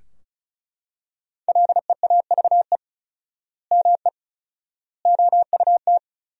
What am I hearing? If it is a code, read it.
Morse code, heard as a keyed tone, about 34 wpm, 700 Hz. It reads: LEAVE ME OUT
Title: 4x Word-Spacing - Sentences from Top 400 Words - 35wpm